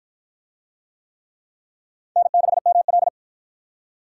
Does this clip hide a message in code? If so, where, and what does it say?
Yes: Morse code from 2.2 to 3.1 s, heard as a keyed tone, about 53 wpm, 700 Hz. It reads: N6ML